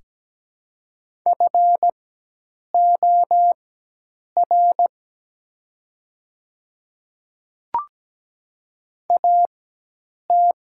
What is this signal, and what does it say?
Morse code, heard as a keyed tone, about 17 wpm, spaced as for 8 wpm, 700 Hz. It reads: FOR AT
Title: 4x Character Spacing - Top 100 Words - 17wpm